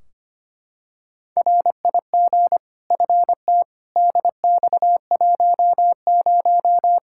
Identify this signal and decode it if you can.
Morse code, heard as a keyed tone, about 25 wpm, 700 Hz. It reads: RIG FT DX10
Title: Practice QSO - Exchange 2 - Info - 25wpm